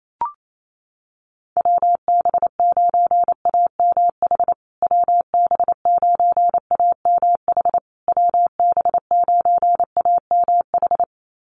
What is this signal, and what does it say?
Morse code, heard as a keyed tone, about 28 wpm, 700 Hz. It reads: WB9AMH WB9AMH WB9AMH
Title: USA Call Signs Repeated 3x 28wpm